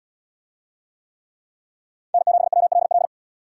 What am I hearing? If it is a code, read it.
Morse code, heard as a keyed tone, about 73 wpm, 700 Hz. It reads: N7WLL